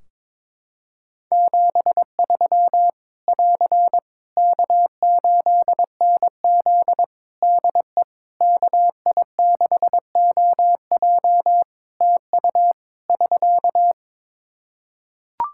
Morse code, heard as a keyed tone, about 21 wpm, 700 Hz. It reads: 73 <AR> K8NZ DE KI6OJ TU <SK>
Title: Practice QSO - Exchange 4 - Ending - 22wpm